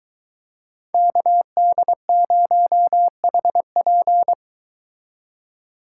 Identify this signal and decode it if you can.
Morse code, heard as a keyed tone, about 23 wpm, 700 Hz. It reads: KD0HP